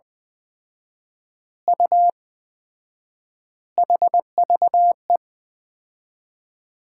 Morse code, heard as a keyed tone, about 20 wpm, 700 Hz. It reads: U HVE
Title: Common QSO Phrases - 4x Word-Spacing - 20wpm